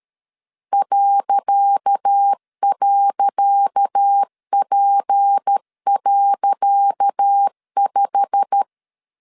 Morse code, heard as a keyed tone, about 13 wpm, 800 Hz. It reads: ..P.5
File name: LEZIONI / 10 SIGNS / 10C DOT 12wpm 2451